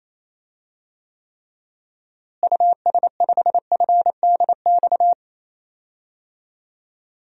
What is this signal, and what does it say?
Morse code, heard as a keyed tone, about 28 wpm, 700 Hz. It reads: US5FDX
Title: DXCC-100 Call Signs - 6 Characters - Repeated 2x - 28wpm